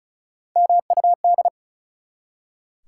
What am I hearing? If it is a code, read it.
Morse code, heard as a keyed tone, about 36 wpm, 700 Hz. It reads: MUD
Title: Three Letter Words - 35wpm